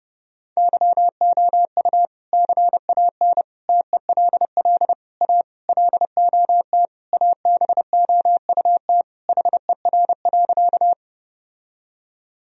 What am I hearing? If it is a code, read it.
Morse code, heard as a keyed tone, about 30 wpm, 700 Hz. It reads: YOU CAN TELL A LOT ABOUT HER.
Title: Sentences from Top 1000 Words - Review Part 3 - 30wpm